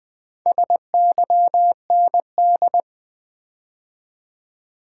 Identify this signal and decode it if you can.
Morse code, heard as a keyed tone, about 20 wpm, 700 Hz. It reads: SYND
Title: Build A Word - 2 Letters At A Time - 8-Letter Words - 20wpm